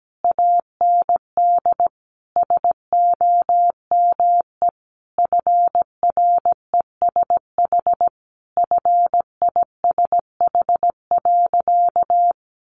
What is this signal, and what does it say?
Morse code, heard as a keyed tone, about 17 wpm, 700 Hz. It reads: AND SOME FRESH FISH.
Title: Sentences from Top 1500 Words 17wpm